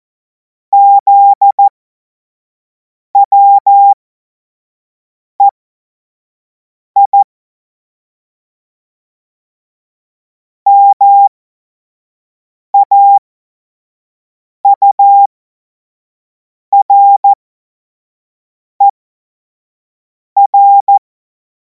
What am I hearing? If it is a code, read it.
Morse code, heard as a keyed tone, about 14 wpm, spaced as for 5 wpm, 800 Hz. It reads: ZWEI MAURER
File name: Morsekurs / Witze / Witz 03. Februar 2026 25 F70